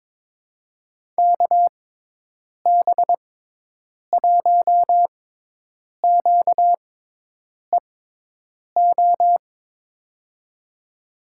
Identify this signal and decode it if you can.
Morse code, heard as a keyed tone, about 22 wpm, spaced as for 8 wpm, 700 Hz. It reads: KB1QEO
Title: USA Call Signs - 6 Characters - 6x Character Spacing - 22wpm